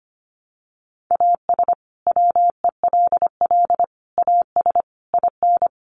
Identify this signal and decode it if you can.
Morse code, heard as a keyed tone, about 25 wpm, 700 Hz. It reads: AS WELL AS IN
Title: Sets of 5 Words Encore 25wpm